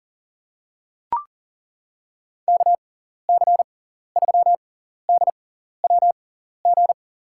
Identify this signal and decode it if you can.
Morse code, heard as a keyed tone, about 40 wpm, spaced as for 14 wpm, 700 Hz. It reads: KC3DWG
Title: USA Call Signs - 6 Characters - 6x Character Spacing - 40wpm